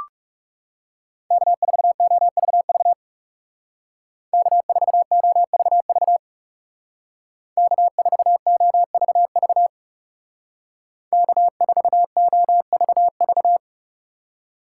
Morse code, heard as a keyed tone, about 36 wpm, 700 Hz. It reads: K4OVV K4OVV K4OVV K4OVV
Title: Speed Racing - USA 1x3 Call Signs 30wpm